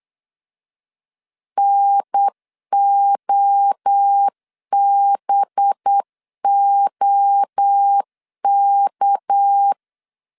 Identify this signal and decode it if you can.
Morse code, heard as a keyed tone, about 8 wpm, 800 Hz. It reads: NOBOK
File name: 8wpm